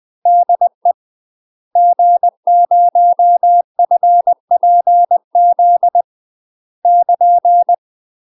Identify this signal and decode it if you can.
Morse code, heard as a keyed tone, about 20 wpm, 700 Hz. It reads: DE G0FPZ <KN>
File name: UK MASTER QSOs TRAINING / 20WPM / UK MASTER QSOs 0053